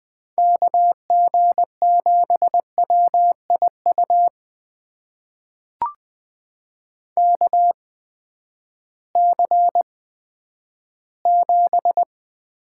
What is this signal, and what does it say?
Morse code, heard as a keyed tone, about 20 wpm, spaced as for 11 wpm, 700 Hz. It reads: KG7WIU KC7
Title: USA Call Signs - 6 Characters - 8x Character Spacing - 20wpm